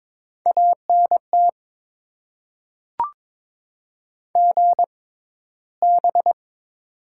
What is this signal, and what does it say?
Morse code, heard as a keyed tone, about 22 wpm, spaced as for 13 wpm, 700 Hz. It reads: ANT GB
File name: Top 90 QSO Elements - 6x Character Spacing - 22wpm